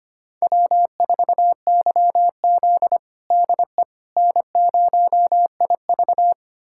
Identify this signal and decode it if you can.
Morse code, heard as a keyed tone, about 25 wpm, 700 Hz. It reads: W4YZ DE N0IV
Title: USA Callsign DE Callsign - 4 Characters - 25wpm